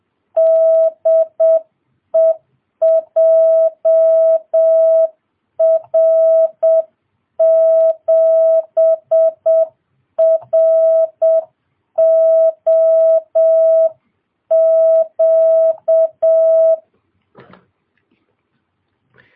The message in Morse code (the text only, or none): DE JR7ROQ